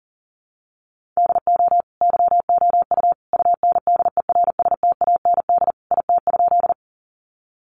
Morse code, heard as a keyed tone, about 38 wpm, 700 Hz. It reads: DO YOU UNDERSTAND IT?
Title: Sentences from Top 600 Words 40wpm